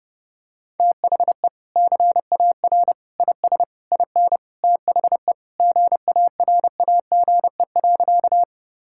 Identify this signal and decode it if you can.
Morse code, heard as a keyed tone, about 30 wpm, 700 Hz. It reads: THE CAR IS IN THE GARAGE.
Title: Sentences from Top 1600 Words 30wpm